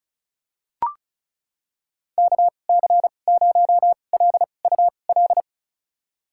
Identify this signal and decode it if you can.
Morse code, heard as a keyed tone, about 36 wpm, spaced as for 25 wpm, 700 Hz. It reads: KC0LUL
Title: USA Call Signs - 6 Characters - 2x Character Spacing - 35wpm